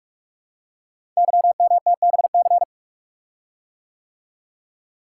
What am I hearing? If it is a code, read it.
Morse code, heard as a keyed tone, about 44 wpm, 690 Hz. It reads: YMTBC
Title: Code Groups - 5 Letters - 45wpm